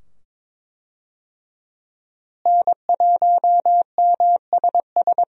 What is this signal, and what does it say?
Morse code, heard as a keyed tone, about 22 wpm, 700 Hz. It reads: N1MSS
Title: USA Call Signs - 5 Characters - Repeated 2x - 22wpm